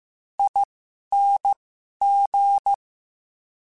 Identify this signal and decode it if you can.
Morse code, heard as a keyed tone, about 15 wpm, spaced as for 11 wpm, 800 Hz. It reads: ING
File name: Special INGs-mp3s / Special INGs-mp3s / SlowToFaster ing @10